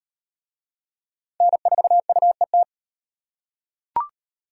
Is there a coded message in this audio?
Yes: Morse code from 1.4 to 2.6 s, heard as a keyed tone, about 38 wpm, 700 Hz. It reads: N4UET